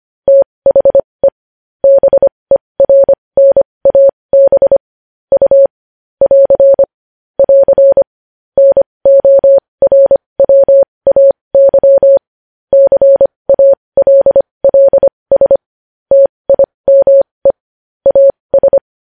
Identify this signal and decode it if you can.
Morse code, heard as a keyed tone, about 25 wpm, spaced as for 20 wpm, 550 Hz. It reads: THE BERNAB U <AR> <AR> NORWAY CALLS TIME AS
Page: news-headlines-in-morse-code-at-20-wpm